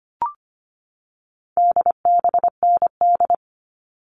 Morse code, heard as a keyed tone, about 25 wpm, 700 Hz. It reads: DBND
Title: All Cut Numbers - 4-Digit 25wpm